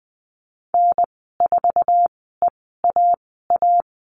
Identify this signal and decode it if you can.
Morse code, heard as a keyed tone, about 19 wpm, spaced as for 14 wpm, 700 Hz. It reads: N4EAA
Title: 069 Lesson 12g - Letter d - Call Signs 20wpm